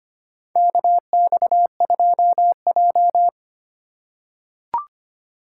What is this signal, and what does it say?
Morse code, heard as a keyed tone, about 25 wpm, 700 Hz. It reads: KX2J
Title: USA Call Signs - 4 Characters - 4x Character Spacing - 25wpm